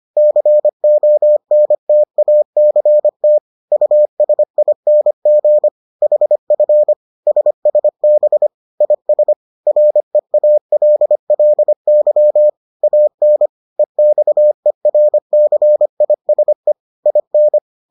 Morse code, heard as a keyed tone, about 25 wpm, 600 Hz. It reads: CONTACT USING HF SSB IS REALLY AN EXERCISE IN